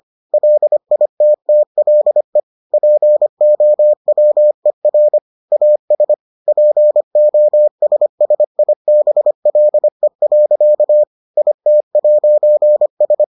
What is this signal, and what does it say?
Morse code, heard as a keyed tone, about 25 wpm, 600 Hz. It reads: LITTLE POWER AS POSSIBLE. IT'S